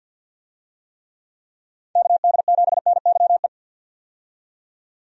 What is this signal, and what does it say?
Morse code, heard as a keyed tone, about 51 wpm, 700 Hz. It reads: KD7NYE
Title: USA Call Signs - 6 Characters - 50wpm